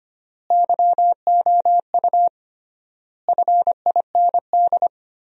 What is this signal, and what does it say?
Morse code, heard as a keyed tone, about 25 wpm, 700 Hz. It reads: YOU FIND
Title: 3x Word-Spacing - Sentences from Top 100 Words - 25wpm